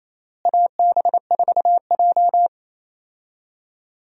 Morse code, heard as a keyed tone, about 28 wpm, 700 Hz. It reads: AB4J
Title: USA Call Signs - 4 Characters - Repeated 3x - 28wpm